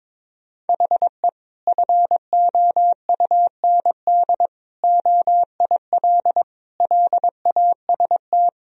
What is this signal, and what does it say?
Morse code, heard as a keyed tone, about 22 wpm, 700 Hz. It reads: HE FOUND OIL LAST
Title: Sentences from Top 1000 Words - Review Part 4 - 22wpm